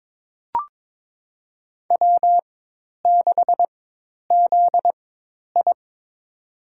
Morse code, heard as a keyed tone, about 22 wpm, spaced as for 10 wpm, 700 Hz. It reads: W6ZI
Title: USA Call Signs - 4 Characters - 4x Character Spacing - 22wpm